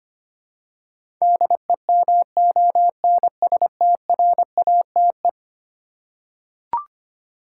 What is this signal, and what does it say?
Morse code, heard as a keyed tone, about 25 wpm, 700 Hz. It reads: DEMONSTRATE
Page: morse-code-ninja-practice-25wpm